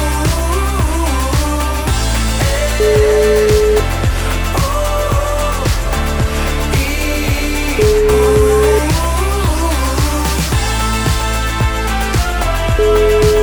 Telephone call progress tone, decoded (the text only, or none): ringback tone